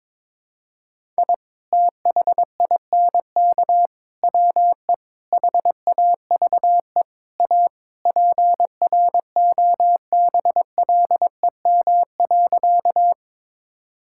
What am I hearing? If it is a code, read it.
Morse code, heard as a keyed tone, about 22 wpm, 700 Hz. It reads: I THINK WE HAVE A PROBLEM.
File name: practice / 22wpm / Sentences with 4-Word Phrases - 22wpm